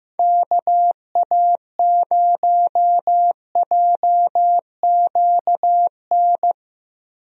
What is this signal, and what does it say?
Morse code, heard as a keyed tone, about 15 wpm, 700 Hz. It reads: KA0JQN
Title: USA Call Signs - 6 Characters - Repeated 3x - 15wpm